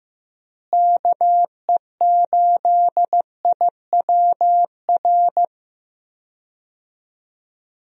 Morse code, heard as a keyed tone, about 15 wpm, 700 Hz. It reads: KE8IWR
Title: USA Call Signs - 6 Characters - Repeated 2x - 15wpm